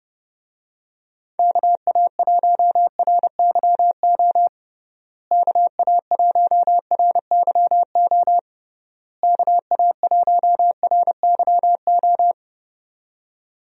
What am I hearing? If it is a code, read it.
Morse code, heard as a keyed tone, about 31 wpm, 700 Hz. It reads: KA1RYO KA1RYO KA1RYO
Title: USA Call Signs - 6 Characters - Repeated 3x - 30wpm